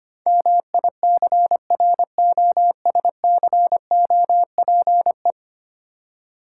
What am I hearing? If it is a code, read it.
Morse code, heard as a keyed tone, about 25 wpm, 700 Hz. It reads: MICROSCOPE
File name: practice / 25wpm / Easy Words with Hints - 25wpm